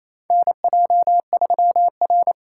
Morse code, heard as a keyed tone, about 28 wpm, 700 Hz. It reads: NJ3R
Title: USA Call Signs - 3x Character Spacing - 28wpm